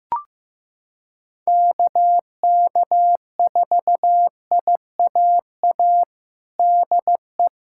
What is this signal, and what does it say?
Morse code, heard as a keyed tone, about 15 wpm, 700 Hz. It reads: KK4IAA DE
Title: USA Callsign DE Callsign - 6 Characters - 15wpm